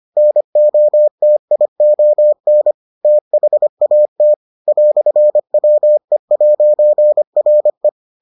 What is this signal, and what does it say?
Morse code, heard as a keyed tone, about 25 wpm, 600 Hz. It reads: NOTION THAT "WE'RE